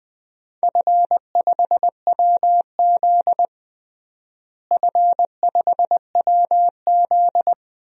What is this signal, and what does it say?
Morse code, heard as a keyed tone, about 20 wpm, 700 Hz. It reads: F5WZ F5WZ
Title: DXCC-100 Call Signs - 4 Characters - Repeated 3x - 20wpm